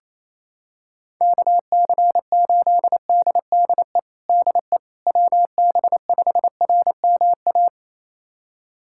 Morse code, heard as a keyed tone, about 28 wpm, 700 Hz. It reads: KC8DDE DE WB5RMA
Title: USA Callsign DE Callsign - 6 Characters - 28wpm